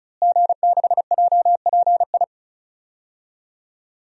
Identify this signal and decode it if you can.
Morse code, heard as a keyed tone, about 36 wpm, 700 Hz. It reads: G6JPI
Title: DXCC-100 Call Signs - 5 Characters - Repeated 3x - 35wpm